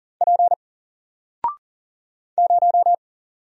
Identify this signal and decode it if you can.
Morse code, heard as a keyed tone, about 40 wpm, 700 Hz. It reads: P 0
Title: Single Character (Letter:Number:Punctuation:Prosign) Alt 40wpm